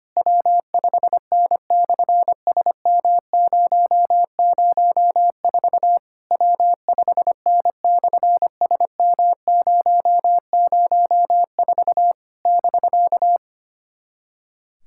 Morse code, heard as a keyed tone, about 25 wpm, 700 Hz. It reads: W5N/SM004 W5N/SM004 <BK>
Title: SOTA QSO Practice 25wpm